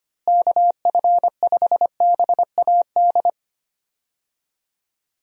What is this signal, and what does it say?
Morse code, heard as a keyed tone, about 25 wpm, 700 Hz. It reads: KF5BAD
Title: USA Call Signs - 6 Characters - Repeated 2x - 25wpm